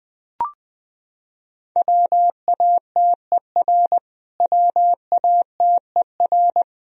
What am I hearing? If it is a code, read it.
Morse code, heard as a keyed tone, about 21 wpm, 700 Hz. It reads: WATER WATER
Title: Top 100 Words Repeated 20wpm